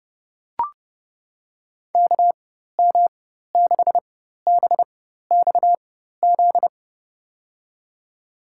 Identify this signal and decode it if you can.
Morse code, heard as a keyed tone, about 32 wpm, spaced as for 14 wpm, 700 Hz. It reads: KM6BXZ
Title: USA Call Signs - 6 Characters - 4x Character Spacing - 30wpm